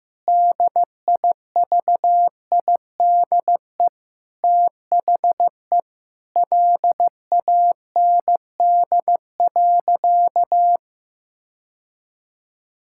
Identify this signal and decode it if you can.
Morse code, heard as a keyed tone, about 15 wpm, 700 Hz. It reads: DIVIDE THE LAND.
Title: Sentences from Top 1600 Words 15wpm